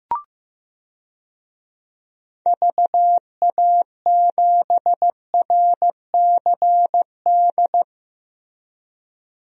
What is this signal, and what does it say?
Morse code, heard as a keyed tone, about 15 wpm, 700 Hz. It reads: VA7RCD
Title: DXCC-100 Call Signs - 6 Characters - 15wpm